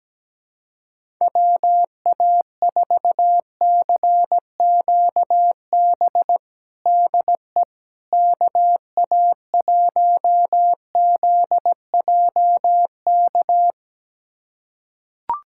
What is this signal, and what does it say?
Morse code, heard as a keyed tone, about 17 wpm, 700 Hz. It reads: WA4CQB DE KA1ZJK
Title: USA Callsign DE Callsign - 6 Characters - 17wpm